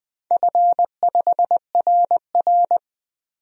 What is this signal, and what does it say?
Morse code, heard as a keyed tone, about 20 wpm, 700 Hz. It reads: F5RR
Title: DXCC-100 Call Signs - 4 Characters - 20wpm